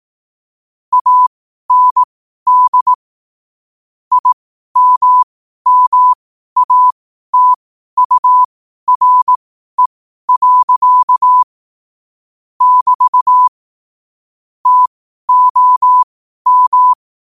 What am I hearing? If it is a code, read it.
Morse code, heard as a keyed tone, about 18 wpm, spaced as for 12 wpm, 1000 Hz. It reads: AND IMMATURE. <BT> TOM
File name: qotdfiles / 12wpm